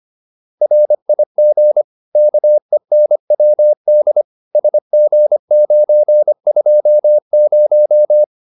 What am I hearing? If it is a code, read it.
Morse code, heard as a keyed tone, about 25 wpm, 600 Hz. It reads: RIG KENWD SG920